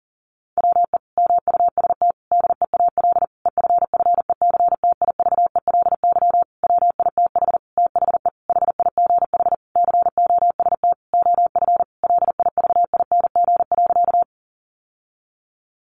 Morse code, heard as a keyed tone, about 40 wpm, 700 Hz. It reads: WE MUST DEAL EFFECTIVELY WITH THE HIGH COST OF LIVING.
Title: Sentences from Top 1400 Words 40wpm